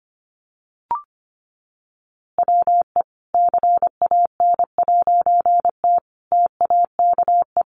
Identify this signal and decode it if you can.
Morse code, heard as a keyed tone, about 25 wpm, 700 Hz. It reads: WE CAN'T TAKE
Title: A Sound Of Thunder 25wpm